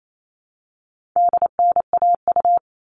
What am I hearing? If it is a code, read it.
Morse code, heard as a keyed tone, about 28 wpm, 700 Hz. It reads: DNAU
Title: All Cut Numbers - 4-Digit 28wpm